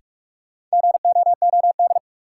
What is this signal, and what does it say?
Morse code, heard as a keyed tone, about 42 wpm, 700 Hz. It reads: GOOD
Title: Sets of 2 Words - New Word Drill - 45wpm